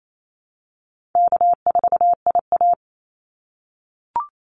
Morse code, heard as a keyed tone, about 28 wpm, 700 Hz. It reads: K4IA
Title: Easy Call Signs - 28wpm